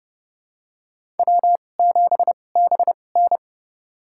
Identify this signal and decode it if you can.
Morse code, heard as a keyed tone, about 30 wpm, spaced as for 22 wpm, 700 Hz. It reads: W7BN
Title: USA Call Signs - 4 Characters - 2x Character Spacing - 30wpm